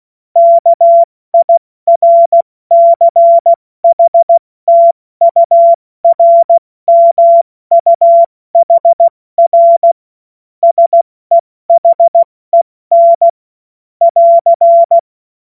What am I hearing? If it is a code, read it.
Morse code, heard as a keyed tone, about 16 wpm, 680 Hz. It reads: KIRCHTURMUHR SEHEN <AR>
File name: Morsekurs / Witze / Witz 24 Juni Mai 2025 70 F80